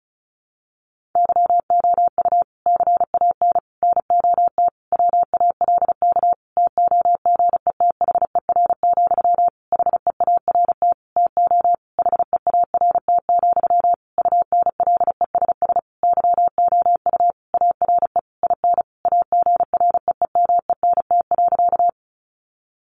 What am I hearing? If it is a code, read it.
Morse code, heard as a keyed tone, about 34 wpm, 700 Hz. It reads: YOU CAN NOT WALK TOGETHER, HEART TO HEART, UNLESS YOU ARE IN AGREEMENT.
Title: Sentences from Top 1500 Words 35wpm